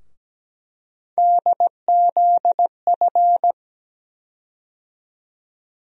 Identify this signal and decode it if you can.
Morse code, heard as a keyed tone, about 17 wpm, 700 Hz. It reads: DZF
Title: USA Call Sign Postfixes - 3 Letters 17wpm